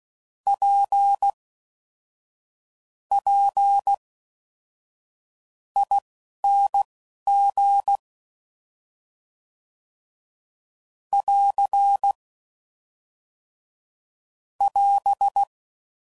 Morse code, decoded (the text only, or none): PPING <AR> <AS>